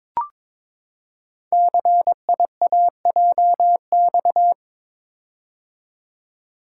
Morse code, heard as a keyed tone, about 22 wpm, 700 Hz. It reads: CIAJX
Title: Code Groups - 5 Letters - 22wpm